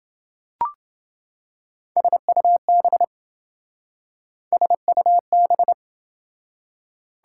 Morse code, heard as a keyed tone, about 27 wpm, 700 Hz. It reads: SUB SUB